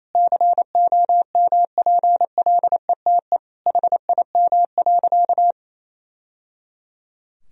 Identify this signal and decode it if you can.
Morse code, heard as a keyed tone, about 28 wpm, 700 Hz. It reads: COMPLETE HIM.
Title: Sentences from Top 1000 Words - Review Part 2 - 28wpm